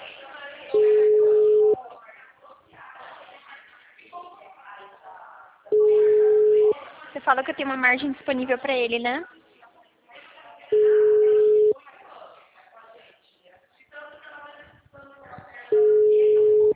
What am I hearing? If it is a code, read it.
Ringback tone (Europe, 425 Hz), with speech.
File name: out- -2005- - - .123